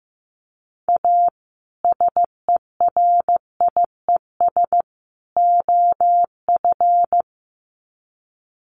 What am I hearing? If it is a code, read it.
Morse code, heard as a keyed tone, about 14 wpm, 700 Hz. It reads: A SERIES OF
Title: Sets of 3 Words 15wpm